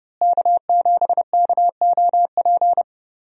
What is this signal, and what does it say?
Morse code, heard as a keyed tone, about 31 wpm, 700 Hz. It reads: K7KOP
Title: USA Call Signs - 5 Characters - Repeated 2x - 30wpm